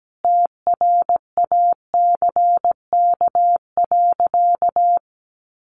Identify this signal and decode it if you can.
Morse code, heard as a keyed tone, about 17 wpm, 700 Hz. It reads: TRACK.
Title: Sentences from Top 1100 Words 17wpm